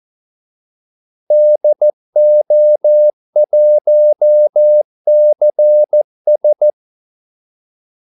Morse code, heard as a keyed tone, about 14 wpm, 600 Hz. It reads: DO1CS
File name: Callsigns-14wpm